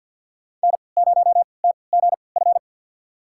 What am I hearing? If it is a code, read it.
Morse code, heard as a keyed tone, about 52 wpm, spaced as for 29 wpm, 700 Hz. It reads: N0TGF
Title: USA Call Signs - 5 Characters - 3x Character Spacing - 50wpm